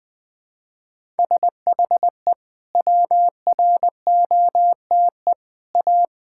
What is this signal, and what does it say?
Morse code, heard as a keyed tone, about 20 wpm, 700 Hz. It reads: SHE WROTE A
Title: Sentences with 2-Word Phrases - Spoken Beginning - 20wpm